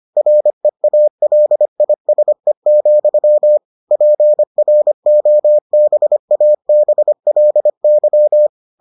Morse code, heard as a keyed tone, about 25 wpm, 600 Hz. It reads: REALISE, PROBABLY